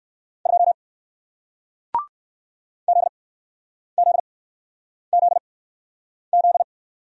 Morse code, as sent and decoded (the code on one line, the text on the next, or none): ..--- / --.. --.. --.. --..
2 ZZZZ